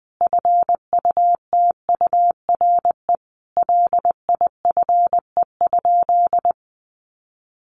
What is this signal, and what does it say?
Morse code, heard as a keyed tone, about 20 wpm, 700 Hz. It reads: FUTURE LIFE?
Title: Sentences from Top 1000 Words 20wpm